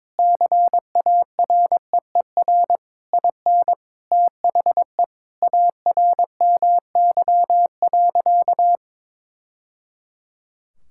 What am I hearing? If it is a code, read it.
Morse code, heard as a keyed tone, about 22 wpm, 700 Hz. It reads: CAREER IN THE ARMY.
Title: Sentences from Top 1000 Words - Review Part 6 - 22wpm